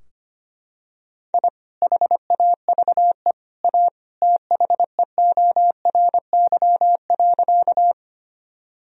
Morse code, heard as a keyed tone, about 25 wpm, 700 Hz. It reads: I HAVE A THEORY.
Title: Sentences from Top 1000 Words - Review Part 3 - 25wpm